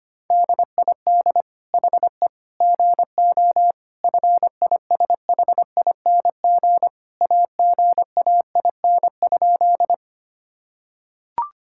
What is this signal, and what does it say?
Morse code, heard as a keyed tone, about 25 wpm, 700 Hz. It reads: DID HE GO FISHING AGAIN?
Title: Sentences from Top 1000 Words - Review Part 5 - 25wpm